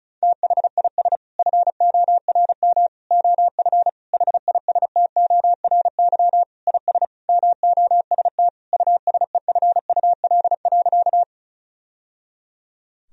Morse code, heard as a keyed tone, about 34 wpm, 700 Hz. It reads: THIS FORM OF HISTORY IS MOST USEFUL.
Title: Sentences from Top 1000 Words - Review Part 4 - 35wpm